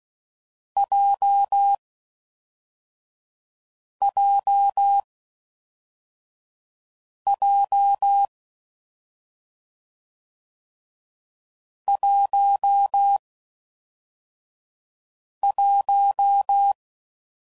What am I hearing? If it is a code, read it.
Morse code, heard as a keyed tone, about 16 wpm, 800 Hz. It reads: J J J 1 1
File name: Combinations / Difficult Combos / AWJ1